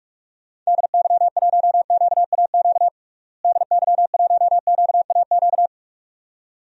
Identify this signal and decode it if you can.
Morse code, heard as a keyed tone, about 46 wpm, 700 Hz. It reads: DY1QAQ DY1QAQ
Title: DXCC-100 Call Signs - 6 Characters - Repeated 2x - 45wpm